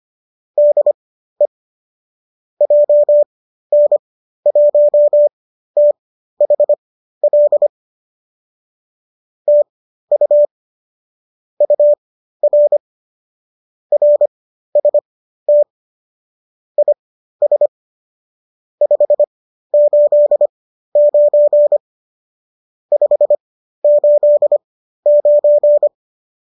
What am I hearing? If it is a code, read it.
Morse code, heard as a keyed tone, about 25 wpm, spaced as for 13 wpm, 590 Hz. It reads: DE JN1THL TU UR RST IS 589 589